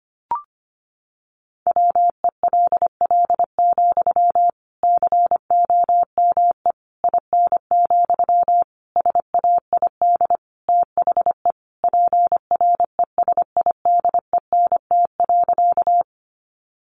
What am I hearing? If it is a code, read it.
Morse code, heard as a keyed tone, about 25 wpm, 700 Hz. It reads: WELL, COME IN, SAID THE PRESIDENT.
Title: Sentences from Top 1000 Words 25wpm